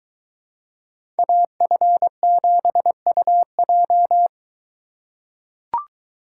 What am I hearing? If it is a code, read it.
Morse code, heard as a keyed tone, about 22 wpm, 700 Hz. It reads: AF7UJ